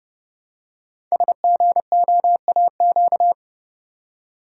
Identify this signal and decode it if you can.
Morse code, heard as a keyed tone, about 30 wpm, 700 Hz. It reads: SGOAQ
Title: Code Groups - 5 Letters - 30wpm